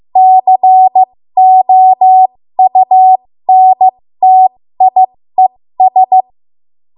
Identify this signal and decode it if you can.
Morse code, heard as a keyed tone, about 15 wpm, spaced as for 13 wpm, 750 Hz. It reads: COUNTIES
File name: Morse / 13WPM